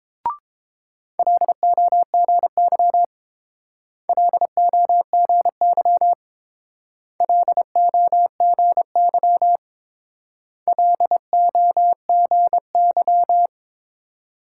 Morse code, heard as a keyed tone, about 28 wpm, 700 Hz. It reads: LOGY LOGY LOGY LOGY